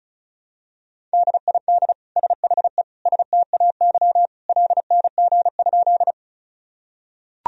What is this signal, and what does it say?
Morse code, heard as a keyed tone, about 34 wpm, 700 Hz. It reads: DID SHE STAY LNG?
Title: Sentences from Top 500 Words and Abbreviations 35wpm